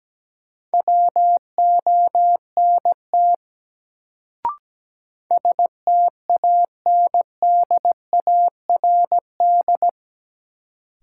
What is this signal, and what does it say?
Morse code, heard as a keyed tone, about 17 wpm, 700 Hz. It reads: WONT STANDARD
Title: Sentences from Top 1000 Words - Review Part 6 - New Word Drill - 17wpm